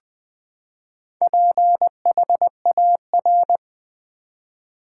Morse code, heard as a keyed tone, about 20 wpm, 700 Hz. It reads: PHAR
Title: Build A Word - 4 Letters At A Time - 8-Letter Words - 20wpm